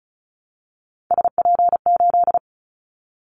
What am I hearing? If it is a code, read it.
Morse code, heard as a keyed tone, about 34 wpm, 700 Hz. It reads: SP8
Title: Callsign Prefixes for Top 25 DXCC - Repeated 3x - 35wpm